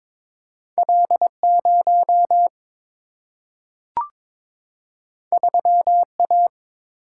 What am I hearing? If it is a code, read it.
Morse code, heard as a keyed tone, about 22 wpm, 700 Hz. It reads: L 0 3 A
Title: Single-Letter Single-Number 22wpm